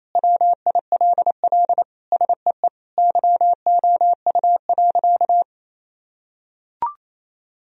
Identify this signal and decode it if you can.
Morse code, heard as a keyed tone, about 28 wpm, 700 Hz. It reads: WILL SEE YOU.
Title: Sentences from Top 1000 Words - Review Part 1 - 28wpm